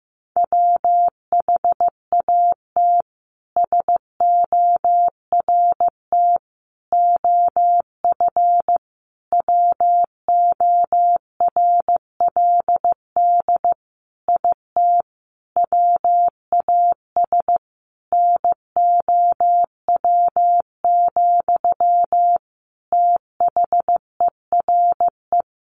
Morse code, heard as a keyed tone, about 15 wpm, 700 Hz. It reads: WHAT SORT OF WORLD IT WAS NOW, THERE
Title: A Sound Of Thunder 15wpm